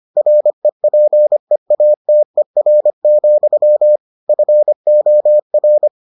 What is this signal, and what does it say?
Morse code, heard as a keyed tone, about 25 wpm, 600 Hz. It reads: REPEATER, FOR